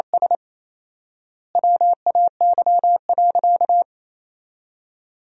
Morse code, heard as a keyed tone, about 28 wpm, 700 Hz. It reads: S WAY.
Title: 4x Word-Spacing - Sentences from Top 200 Words - 28wpm